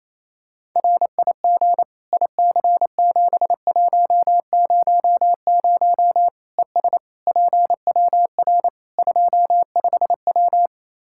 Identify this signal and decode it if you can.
Morse code, heard as a keyed tone, about 27 wpm, 700 Hz. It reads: RIG IC7100 ES PWR 25W
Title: K1USN SST QSO Practice 28wpm